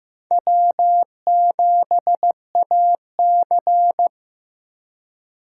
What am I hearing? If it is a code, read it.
Morse code, heard as a keyed tone, about 18 wpm, 700 Hz. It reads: W7AC
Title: Speed Racing - USA 1x2 Call Signs 15wpm